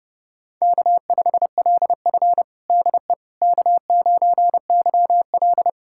Morse code, heard as a keyed tone, about 30 wpm, 700 Hz. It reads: K5LF DE K9YL
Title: USA Callsign DE Callsign - 4 Characters - 30wpm